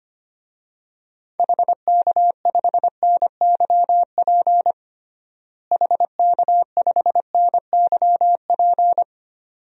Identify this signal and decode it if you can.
Morse code, heard as a keyed tone, about 25 wpm, 700 Hz. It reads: HK5NYP HK5NYP
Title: DXCC-100 Call Signs - 6 Characters - Repeated 2x - 25wpm